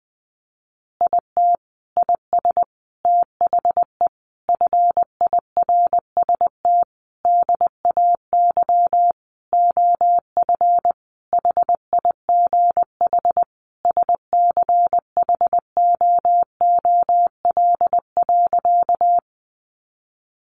Morse code, heard as a keyed tone, about 20 wpm, 700 Hz. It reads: IT IS THE FIRST DAY OF HIGH SCHOOL.
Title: Additional Easy Phrases 20wpm